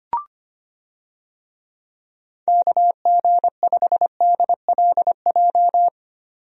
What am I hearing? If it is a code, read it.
Morse code, heard as a keyed tone, about 25 wpm, 700 Hz. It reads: KG5DLJ
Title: USA Call Signs - 6 Characters - 25wpm